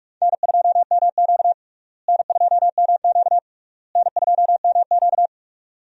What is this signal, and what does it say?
Morse code, heard as a keyed tone, about 45 wpm, 700 Hz. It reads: N2MQ N2MQ N2MQ
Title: USA Call Signs - 4 Characters - Repeated 3x - 45wpm